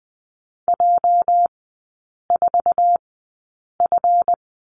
Morse code, heard as a keyed tone, about 20 wpm, spaced as for 8 wpm, 700 Hz. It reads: J4F
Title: 205 Lesson 33e - Letter j - Three Characters 20wpm